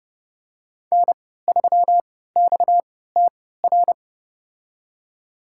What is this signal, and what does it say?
Morse code, heard as a keyed tone, about 30 wpm, spaced as for 17 wpm, 700 Hz. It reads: N3XTR
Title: USA Call Signs - 5 Characters - 3x Character Spacing - 30wpm